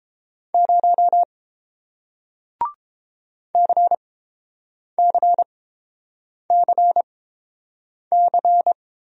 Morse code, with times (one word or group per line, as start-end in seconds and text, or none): 0.5-1.2 0
3.5-8.7 CCCC